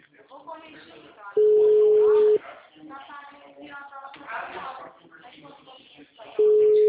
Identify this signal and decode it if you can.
Ringback tone (Europe, 425 Hz), with speech.